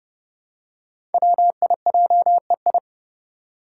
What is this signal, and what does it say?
Morse code, heard as a keyed tone, about 30 wpm, 700 Hz. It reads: WIJEI
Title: Five Challenging Letter-Number Combinations 30wpm